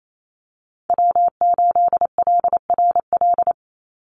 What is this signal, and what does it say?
Morse code, heard as a keyed tone, about 28 wpm, 700 Hz. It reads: W8LRL
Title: USA Call Signs 28wpm